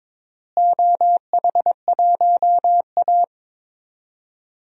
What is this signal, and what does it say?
Morse code, heard as a keyed tone, about 21 wpm, 700 Hz. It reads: OH1A
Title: DXCC-100 Call Signs - 4 Characters - Repeated 3x - 22wpm